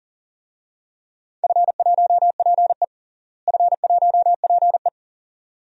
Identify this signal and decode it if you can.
Morse code, heard as a keyed tone, about 40 wpm, 700 Hz. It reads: F1PE F1PE
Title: DXCC-100 Call Signs - 4 Characters - Repeated 2x - 40wpm